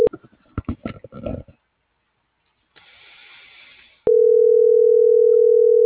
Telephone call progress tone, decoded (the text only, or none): ringback tone